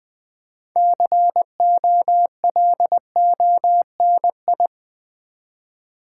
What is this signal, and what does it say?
Morse code, heard as a keyed tone, about 20 wpm, 700 Hz. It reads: COLONI
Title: Build A Word - 2 Letters At A Time - 8-Letter Words - 20wpm